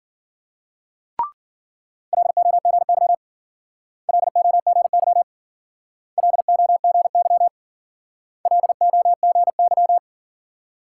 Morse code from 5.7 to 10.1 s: LOGY LOGY

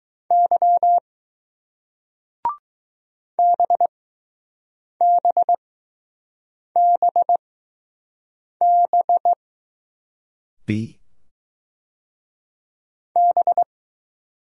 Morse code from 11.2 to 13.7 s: B

